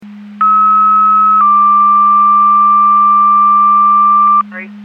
Two-tone page, heard as a one-second tone, then a three-second tone, about 1290 then 1180 Hz. It is heard from 0.4 to 4.4 s.
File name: Portage 154.385 Paging